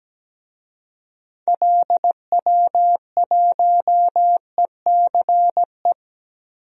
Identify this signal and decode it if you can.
Morse code, heard as a keyed tone, about 17 wpm, 700 Hz. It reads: LW1ECE
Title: DXCC-100 Call Signs - 6 Characters - Repeated 3x - 17wpm